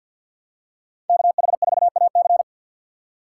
Morse code, heard as a keyed tone, about 52 wpm, 700 Hz. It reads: KH4AC
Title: USA Call Signs - 2x Character Spacing - 50wpm